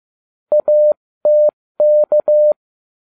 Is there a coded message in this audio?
Yes: Morse code from 0.5 to 2.5 s, heard as a keyed tone, about 15 wpm, 600 Hz. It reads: ATK